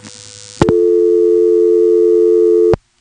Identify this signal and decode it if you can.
Dial tone (North America), 350 + 440 Hz.